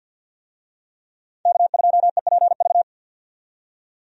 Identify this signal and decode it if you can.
Morse code, heard as a keyed tone, about 52 wpm, 700 Hz. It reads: K2EPV
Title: USA Call Signs - 5 Characters - 50wpm